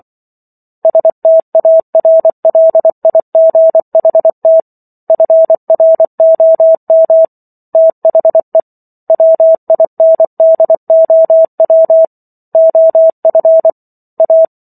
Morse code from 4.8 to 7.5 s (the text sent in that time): FROM